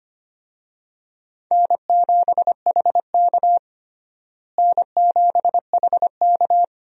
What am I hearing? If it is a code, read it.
Morse code, heard as a keyed tone, about 25 wpm, 700 Hz. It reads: N7HK N7HK
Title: USA Call Signs - 4 Characters - Repeated 2x - 25wpm